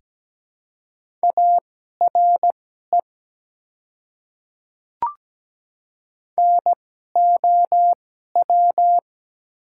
Morse code, heard as a keyed tone, about 17 wpm, spaced as for 12 wpm, 690 Hz. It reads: ARE NOW
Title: 2x Character Spacing - Top 100 Words - 17wpm